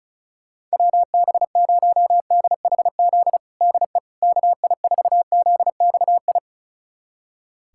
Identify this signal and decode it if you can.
Morse code, heard as a keyed tone, about 34 wpm, 700 Hz. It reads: WB0DHZ DE KI4ZXI
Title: USA Callsign DE Callsign - 6 Characters - 35wpm